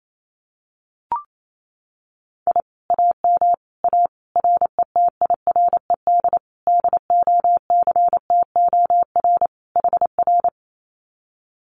Morse code, heard as a keyed tone, about 28 wpm, 700 Hz. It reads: I AM A RETIRED DOCTOR HR
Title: Common QSO Phrases Encore 28wpm